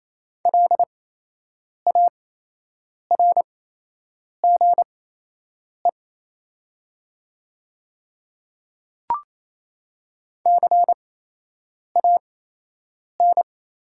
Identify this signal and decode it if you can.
Morse code, heard as a keyed tone, about 28 wpm, spaced as for 8 wpm, 700 Hz. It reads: LARGE CAN
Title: 8x Character Spacing - Top 200 Words - 28wpm